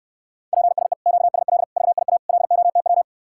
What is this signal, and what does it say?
Morse code, heard as a keyed tone, about 69 wpm, 700 Hz. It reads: PSE QSL VIA LOTW